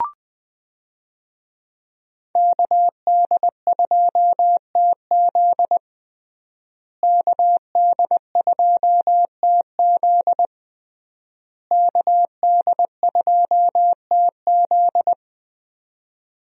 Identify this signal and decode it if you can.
Morse code, heard as a keyed tone, about 21 wpm, 700 Hz. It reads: KD2TZ KD2TZ KD2TZ
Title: USA Call Signs - 5 Characters - Repeated 3x - 20wpm